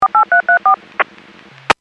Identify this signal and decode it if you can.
DTMF, heard as telephone touch tones, tones of about 110 ms. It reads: 48334